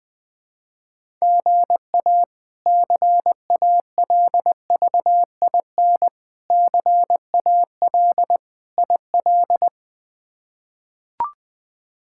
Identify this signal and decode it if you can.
Morse code, heard as a keyed tone, about 21 wpm, 700 Hz. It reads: GA CALVIN CAL IL
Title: K1USN SST QSO Practice - 20wpm